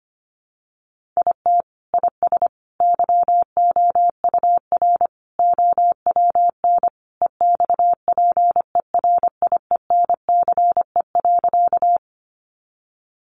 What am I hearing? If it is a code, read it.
Morse code, heard as a keyed tone, about 25 wpm, 700 Hz. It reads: IT IS YOUR OWN EXPERIENCE.
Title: Sentences from Top 600 Words 25wpm